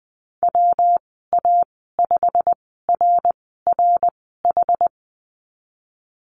Morse code, heard as a keyed tone, about 20 wpm, spaced as for 14 wpm, 700 Hz. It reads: WA5RRH
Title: 140 Lesson 23g - Letter f - Call Signs 20wpm